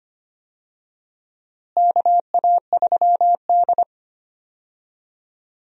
Morse code, heard as a keyed tone, about 25 wpm, 700 Hz. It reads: KA3D
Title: USA Call Signs - 4 Characters - Repeated 2x - 25wpm